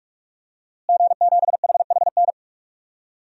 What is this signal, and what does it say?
Morse code, heard as a keyed tone, about 46 wpm, 700 Hz. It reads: G7HHN
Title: DXCC-100 Call Signs - 5 Characters - Repeated 2x - 45wpm